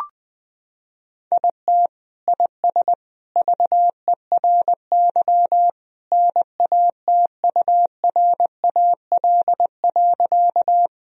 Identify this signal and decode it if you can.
Morse code, heard as a keyed tone, about 20 wpm, 700 Hz. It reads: IT IS VERY NATURAL.
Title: Sentences from Top 1000 Words - Review Part 4 - 20wpm